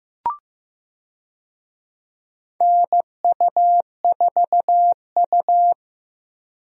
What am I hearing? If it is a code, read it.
Morse code, heard as a keyed tone, about 15 wpm, 700 Hz. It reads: NU4U
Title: USA Call Signs - 4 Characters - Repeated 3x - 15wpm